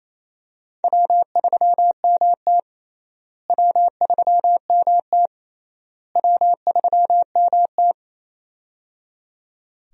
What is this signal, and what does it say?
Morse code, heard as a keyed tone, about 28 wpm, 700 Hz. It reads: W3MT W3MT W3MT
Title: USA Call Signs - 4 Characters - Repeated 3x - 28wpm